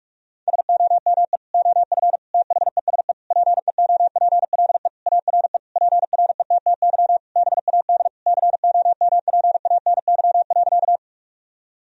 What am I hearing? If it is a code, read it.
Morse code, heard as a keyed tone, about 45 wpm, 700 Hz. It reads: SOME OF THESE PEOPLE ARE PRETTY BAD COMPANY.
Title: Sentences from Top 1000 Words - Review Part 3 - 45wpm